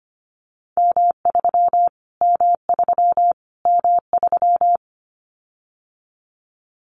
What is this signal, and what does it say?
Morse code, heard as a keyed tone, about 25 wpm, 700 Hz. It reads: M3 M3 M3
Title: Callsign Prefixes for Top 25 DXCC - Repeated 3x - 25wpm